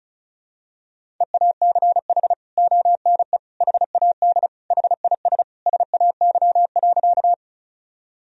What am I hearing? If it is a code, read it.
Morse code, heard as a keyed tone, about 34 wpm, 700 Hz. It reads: EACH ONE HAD HIS SAY.
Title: Sentences from Top 1000 Words - Review Part 1 - 35wpm